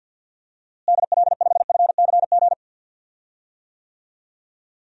Morse code, heard as a keyed tone, about 52 wpm, 700 Hz. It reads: DL5FCG
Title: DXCC-100 Call Signs - 6 Characters - Repeated 2x - 50wpm